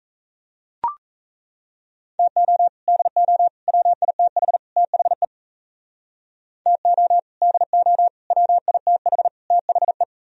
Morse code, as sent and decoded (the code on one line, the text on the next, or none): - --- / -.. --- / .-- .. - .... / - .... / - --- / -.. --- / .-- .. - .... / - .... .
TO DO WITH THE TO DO WITH THE